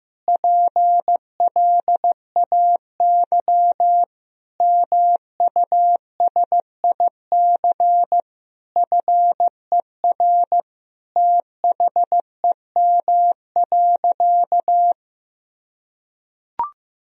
Morse code, heard as a keyed tone, about 15 wpm, 700 Hz. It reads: PLAY MUSIC FER THEM.
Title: Sentences from Top 300 Words and Abbreviations 15wpm